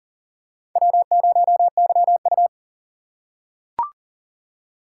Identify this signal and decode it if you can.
Morse code, heard as a keyed tone, about 40 wpm, 700 Hz. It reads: W0YU
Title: USA Call Signs - 4 Characters - 3x Character Spacing - 40wpm